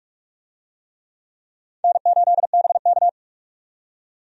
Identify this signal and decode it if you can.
Morse code, heard as a keyed tone, about 45 wpm, 700 Hz. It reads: N8BK
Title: USA Call Signs - 4 Characters - 45wpm